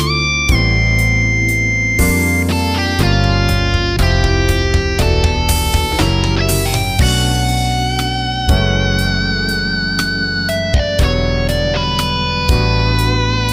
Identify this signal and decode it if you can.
Dial-up modem handshake.